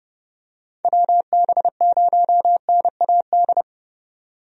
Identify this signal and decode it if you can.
Morse code, heard as a keyed tone, about 30 wpm, 700 Hz. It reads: WB0NAD
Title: USA Call Signs - 6 Characters - 30wpm